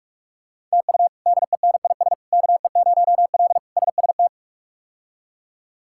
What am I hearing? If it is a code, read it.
Morse code, heard as a keyed tone, about 43 wpm, 700 Hz. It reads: TU DENIS KE0L SST
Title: K1USN SST QSO Practice - 45wpm